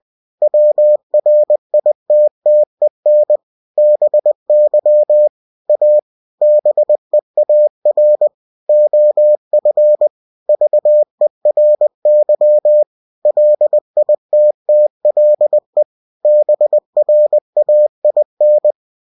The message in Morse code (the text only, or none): WRITTEN BY A BEAR OF VERY LITTLE BRAIN